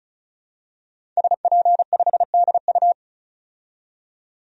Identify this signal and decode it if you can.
Morse code, heard as a keyed tone, about 34 wpm, 700 Hz. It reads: SP5DU
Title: DXCC-100 Call Signs - 5 Characters - 35wpm